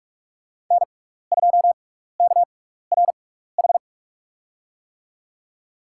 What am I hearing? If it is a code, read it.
Morse code, heard as a keyed tone, about 46 wpm, spaced as for 16 wpm, 700 Hz. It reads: N2KRH
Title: USA Call Signs - 5 Characters - 6x Character Spacing - 45wpm